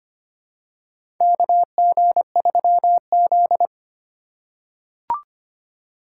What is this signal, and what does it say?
Morse code, heard as a keyed tone, about 25 wpm, 700 Hz. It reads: KG3Z
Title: USA Call Signs - 4 Characters - 4x Character Spacing - 25wpm